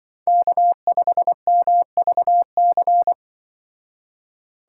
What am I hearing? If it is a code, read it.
Morse code, heard as a keyed tone, about 25 wpm, 700 Hz. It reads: K5MVC
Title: Speed Racing - USA Call Signs 20wpm